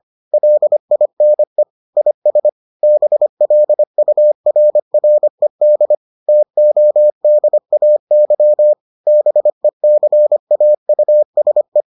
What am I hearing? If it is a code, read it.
Morse code, heard as a keyed tone, about 25 wpm, 600 Hz. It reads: LINE IS BLURRED TODAY BECAUSE